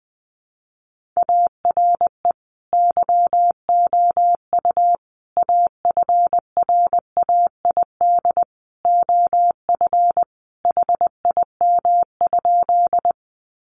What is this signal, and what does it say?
Morse code, heard as a keyed tone, about 20 wpm, 700 Hz. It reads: ARE YOU AFRAID OF HIM?
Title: Sentences from Top 1500 Words 20wpm